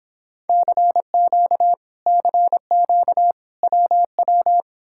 Morse code, heard as a keyed tone, about 30 wpm, 700 Hz. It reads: CQ CQ WW